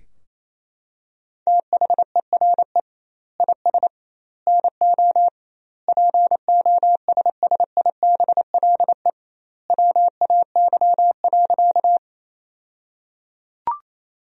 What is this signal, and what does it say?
Morse code, heard as a keyed tone, about 27 wpm, 700 Hz. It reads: THERE IS NO POSSIBLE WAY.
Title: 2x Word-Spacing - Sentences from Top 500 Words - 28wpm